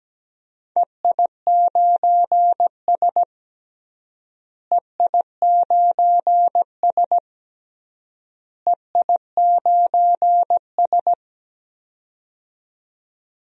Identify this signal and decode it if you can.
Morse code, heard as a keyed tone, about 17 wpm, 700 Hz. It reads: EI9S EI9S EI9S
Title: DXCC-100 Call Signs - 4 Characters - Repeated 3x - 17wpm